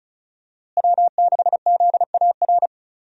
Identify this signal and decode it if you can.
Morse code, heard as a keyed tone, about 36 wpm, 700 Hz. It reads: W6ZAR
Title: USA Call Signs - 5 Characters - Repeated 2x - 35wpm